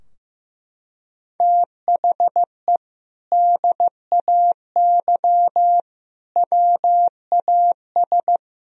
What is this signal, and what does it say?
Morse code, heard as a keyed tone, about 15 wpm, 700 Hz. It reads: THE DAY WAS
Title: Sentences from Top 1000 Words - Review Part 2 - 15wpm